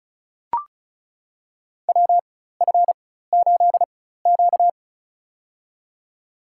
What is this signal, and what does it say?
Morse code, heard as a keyed tone, about 36 wpm, spaced as for 16 wpm, 700 Hz. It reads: WF8Q
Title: USA Call Signs - 4 Characters - 4x Character Spacing - 35wpm